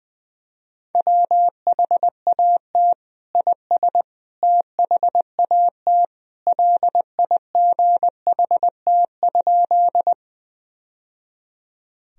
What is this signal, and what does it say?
Morse code, heard as a keyed tone, about 21 wpm, 700 Hz. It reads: WHAT IS THAT LIGHT?
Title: Sentences from Top 1000 Words - Review Part 1 - 20wpm